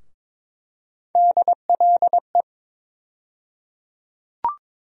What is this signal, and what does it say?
Morse code, heard as a keyed tone, about 21 wpm, 700 Hz. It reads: DLE
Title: USA Call Sign Postfixes - 3 Letters 22wpm